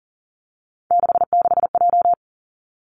Morse code, heard as a keyed tone, about 40 wpm, 700 Hz. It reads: 66J
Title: Three Challenging Letter-Number Combinations 40wpm